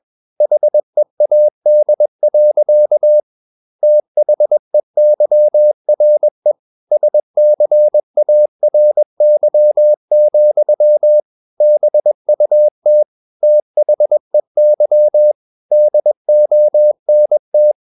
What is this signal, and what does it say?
Morse code, heard as a keyed tone, about 21 wpm, 600 Hz. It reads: HEAD. THEYRE SCARY, BUT THEY DONT
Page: CW med Gnister